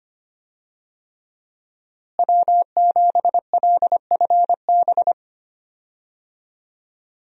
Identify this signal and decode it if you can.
Morse code, heard as a keyed tone, about 25 wpm, 700 Hz. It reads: W7LFB
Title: USA Call Signs - 5 Characters - Repeated 3x - 25wpm